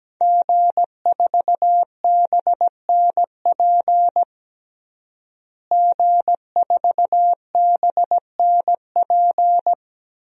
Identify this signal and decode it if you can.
Morse code, heard as a keyed tone, about 17 wpm, 700 Hz. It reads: G4BNP G4BNP
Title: DXCC-100 Call Signs - 5 Characters - Repeated 3x - 17wpm